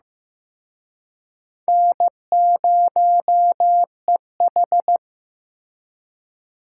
Morse code, heard as a keyed tone, about 18 wpm, 700 Hz. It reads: N0EH
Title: Speed Racing - USA 1x2 Call Signs 15wpm